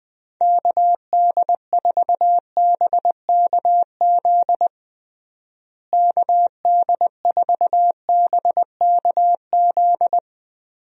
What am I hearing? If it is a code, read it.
Morse code, heard as a keyed tone, about 20 wpm, 700 Hz. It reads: KD4BKZ KD4BKZ
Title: USA Call Signs - 6 Characters - Repeated 3x - 20wpm